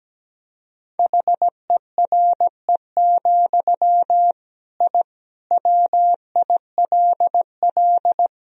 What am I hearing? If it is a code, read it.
Morse code, heard as a keyed tone, about 17 wpm, 700 Hz. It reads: HERE, I WILL
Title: Sentences from Top 1000 Words - Review Part 1 - 17wpm